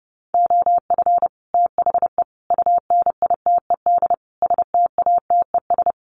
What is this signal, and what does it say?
Morse code, heard as a keyed tone, about 30 wpm, 700 Hz. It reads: OF THE UNITED STATES
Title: Sets of 4 Words 30wpm